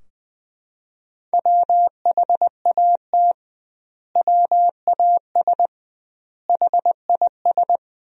Morse code, heard as a keyed tone, about 21 wpm, 700 Hz. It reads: WHAT WAS HIS
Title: 2x Word-Spacing - Sentences from Top 200 Words - 20wpm